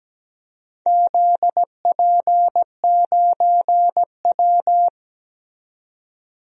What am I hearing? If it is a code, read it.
Morse code, heard as a keyed tone, about 17 wpm, 700 Hz. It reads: ZP9W
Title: DXCC-100 Call Signs - 4 Characters - 17wpm